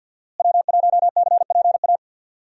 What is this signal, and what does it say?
Morse code, heard as a keyed tone, about 52 wpm, 700 Hz. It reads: W1CPA
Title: USA Call Signs - 5 Characters - Repeated 2x - 50wpm